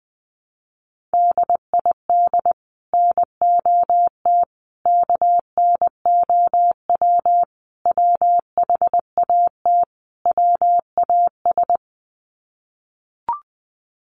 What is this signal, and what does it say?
Morse code, heard as a keyed tone, about 20 wpm, 700 Hz. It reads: DID NOT KNOW WHAT WAS
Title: Sets of 5 Words Encore2 20wpm